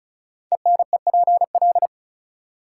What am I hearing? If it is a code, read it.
Morse code, heard as a keyed tone, about 36 wpm, 700 Hz. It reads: ENEPL
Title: Code Groups - 5 Letters - 35wpm